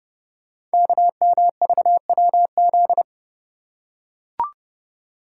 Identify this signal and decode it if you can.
Morse code, heard as a keyed tone, about 29 wpm, 700 Hz. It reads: KMVWZ
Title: Code Groups - 5 Letters - 30wpm